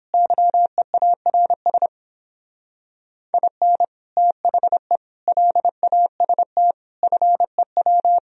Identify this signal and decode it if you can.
Morse code, heard as a keyed tone, about 31 wpm, 700 Hz. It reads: YEARS IN THE LAST FEW